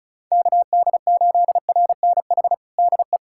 Morse code, heard as a keyed tone, about 36 wpm, 700 Hz. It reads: KD8RNH DE
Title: USA Callsign DE Callsign 35wpm